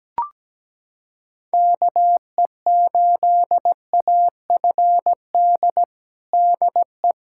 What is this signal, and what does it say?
Morse code, heard as a keyed tone, about 17 wpm, 700 Hz. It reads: KE8AFD DE
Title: USA Callsign DE Callsign 17wpm